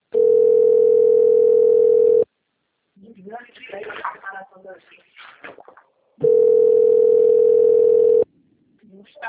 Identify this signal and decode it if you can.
Ringback tone (North America), 440 + 480 Hz, with speech.
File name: rg-1600-unknown- - - .36386